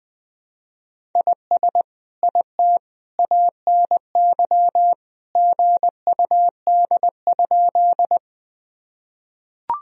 Morse code, heard as a keyed tone, about 20 wpm, 700 Hz. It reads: IS IT ANY GUD?
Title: Sentences from Top 200 Words and Abbreviations 20wpm